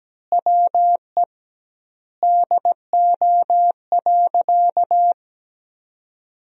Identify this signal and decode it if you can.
Morse code, heard as a keyed tone, about 17 wpm, 700 Hz. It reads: WE DO.
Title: 2x Word-Spacing - Sentences from Top 100 Words - 17wpm